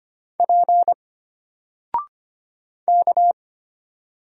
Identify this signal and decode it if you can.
Morse code, heard as a keyed tone, about 25 wpm, 700 Hz. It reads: P K
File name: Single-Letter-Number-25wpm